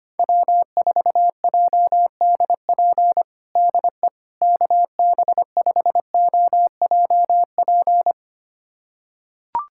Morse code, heard as a keyed tone, about 25 wpm, 700 Hz. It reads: W4JDP DE KB5OJP
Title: USA Callsign DE Callsign 25wpm